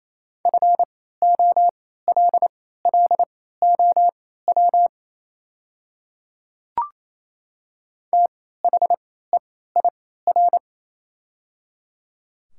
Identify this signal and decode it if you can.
Morse code, heard as a keyed tone, about 29 wpm, spaced as for 16 wpm, 700 Hz. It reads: FOLLOW THEIR
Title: 3x Character Spacing - Top 300 Words - 28wpm